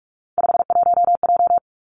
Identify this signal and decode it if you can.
Morse code, heard as a keyed tone, about 46 wpm, 700 Hz. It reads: 51J
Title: Three Challenging Letter-Number Combinations 45wpm